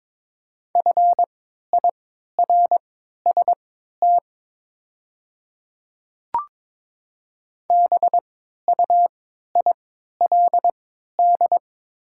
Morse code, heard as a keyed tone, about 22 wpm, spaced as for 12 wpm, 700 Hz. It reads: FIRST BUILD
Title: 3x Character Spacing - Top 200 Words - 22wpm